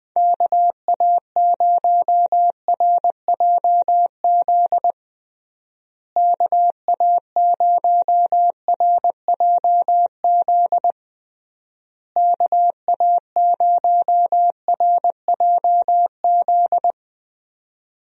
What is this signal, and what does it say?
Morse code, heard as a keyed tone, about 20 wpm, 700 Hz. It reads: KA0RJZ KA0RJZ KA0RJZ
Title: USA Call Signs - 6 Characters - Repeated 3x - 20wpm